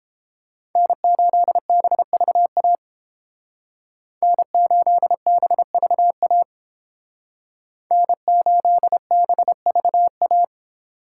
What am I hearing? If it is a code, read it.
Morse code, heard as a keyed tone, about 28 wpm, 700 Hz. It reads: N8BVA N8BVA N8BVA